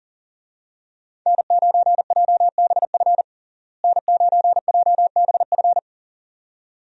Morse code, heard as a keyed tone, about 40 wpm, 700 Hz. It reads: N9JBF N9JBF
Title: USA Call Signs - 5 Characters - Repeated 2x - 40wpm